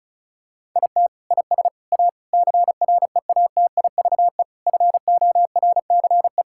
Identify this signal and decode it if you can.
Morse code, heard as a keyed tone, about 36 wpm, 690 Hz. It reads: IT IS A CREATIVE FORCE
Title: Sentences from Top 1000 Words - Review Part 5 - 35wpm